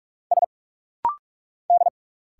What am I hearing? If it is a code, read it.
Morse code, heard as a keyed tone, about 46 wpm, 700 Hz. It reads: S D
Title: Single Letters - Rapid-Fire 45wpm